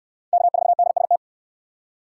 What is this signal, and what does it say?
Morse code, heard as a keyed tone, about 60 wpm, 700 Hz. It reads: K4DST